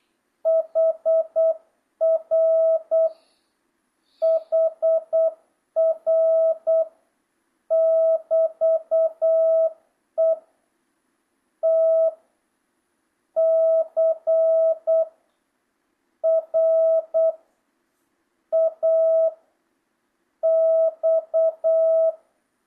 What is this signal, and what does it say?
Morse code, heard as a keyed tone, about 8 wpm, 650 Hz. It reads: HR HR <BT> E T C R A X
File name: jh8cbh / 2025cw